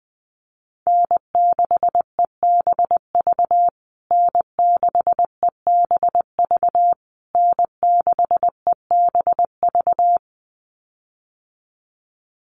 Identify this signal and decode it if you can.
Morse code, heard as a keyed tone, about 19 wpm, 700 Hz. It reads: N6EBV N6EBV N6EBV
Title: USA Call Signs Repeated 3x 20wpm